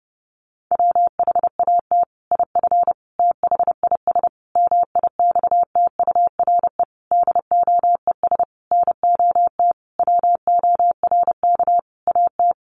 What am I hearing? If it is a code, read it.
Morse code, heard as a keyed tone, about 30 wpm, 700 Hz. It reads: WHAT IF THIS MIXTURE DOES NOT WORK AT
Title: Sentences from Top 1500 Words 30wpm